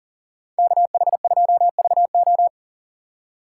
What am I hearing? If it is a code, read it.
Morse code, heard as a keyed tone, about 40 wpm, 700 Hz. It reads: KH2VO
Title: USA Call Signs - 5 Characters - 2x Character Spacing - 40wpm